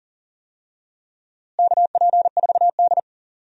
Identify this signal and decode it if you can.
Morse code, heard as a keyed tone, about 39 wpm, 700 Hz. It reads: KP4D
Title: DXCC-100 Call Signs - 4 Characters - Repeated 2x - 40wpm